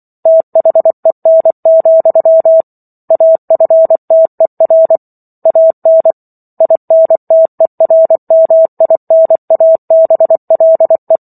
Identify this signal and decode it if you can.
Morse code, heard as a keyed tone, about 24 wpm, 650 Hz. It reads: THEN, AFTER AN INTERMINABLE